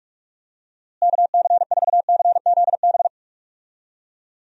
Morse code, heard as a keyed tone, about 46 wpm, 700 Hz. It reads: KC4CZB
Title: USA Call Signs - 6 Characters - Repeated 2x - 45wpm